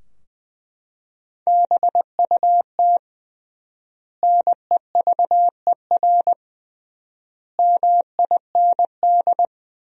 Morse code, heard as a keyed tone, about 21 wpm, 700 Hz. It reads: BUT NEVER MIND
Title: 3x Word-Spacing - Sentences from Top 500 Words - 20wpm